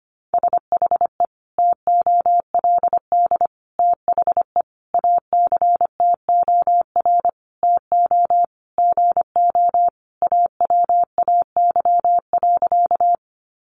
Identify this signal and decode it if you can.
Morse code, heard as a keyed tone, about 25 wpm, 700 Hz. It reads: SHE TOLD THE ACTOR TO GO AWAY.
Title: Sentences from Top 1500 Words 25wpm